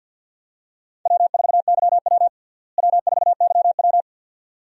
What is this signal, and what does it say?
Morse code, heard as a keyed tone, about 52 wpm, 700 Hz. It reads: W4YW W4YW
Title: USA Call Signs - 4 Characters - Repeated 2x - 50wpm